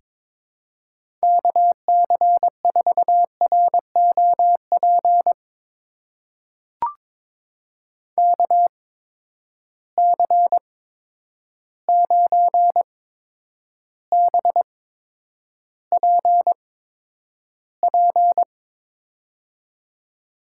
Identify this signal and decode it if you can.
Morse code, heard as a keyed tone, about 22 wpm, spaced as for 9 wpm, 700 Hz. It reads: KC4ROP KC9BPP
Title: USA Call Signs - 6 Characters - 8x Character Spacing - 22wpm